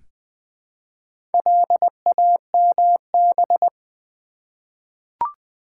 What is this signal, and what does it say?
Morse code, heard as a keyed tone, about 20 wpm, 700 Hz. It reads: LAMB
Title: Easy Words with Hints - 20wpm